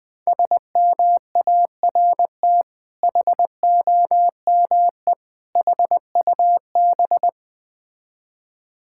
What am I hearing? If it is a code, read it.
Morse code, heard as a keyed tone, about 20 wpm, 690 Hz. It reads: SMART HOME HUB
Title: Difficult Words with Hints - 20wpm